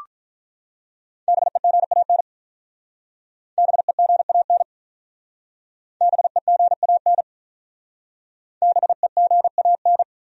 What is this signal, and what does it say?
Morse code, heard as a keyed tone, about 48 wpm, 700 Hz. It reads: BEGAN BEGAN BEGAN BEGAN